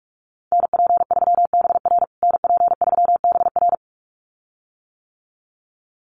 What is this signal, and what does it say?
Morse code, heard as a keyed tone, about 44 wpm, 700 Hz. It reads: NP3BR NP3BR
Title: USA Call Signs Repeated 2x 45wpm